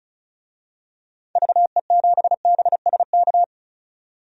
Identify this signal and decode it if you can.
Morse code, heard as a keyed tone, about 36 wpm, 700 Hz. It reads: VE7BSK
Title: DXCC-100 Call Signs - 6 Characters - 35wpm